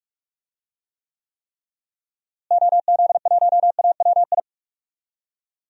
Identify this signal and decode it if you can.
Morse code, heard as a keyed tone, about 46 wpm, 700 Hz. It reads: OZ1AWI
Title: DXCC-100 Call Signs - 6 Characters - Repeated 3x - 45wpm